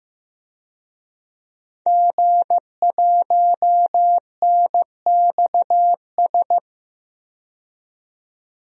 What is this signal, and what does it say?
Morse code, heard as a keyed tone, about 15 wpm, 700 Hz. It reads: G1NXS
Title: DXCC-100 Call Signs - 5 Characters - 15wpm